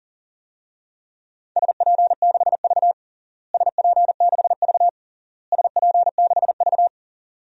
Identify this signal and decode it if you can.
Morse code, heard as a keyed tone, about 37 wpm, 700 Hz. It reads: SP6V SP6V SP6V
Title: DXCC-100 Call Signs - 4 Characters - Repeated 3x - 40wpm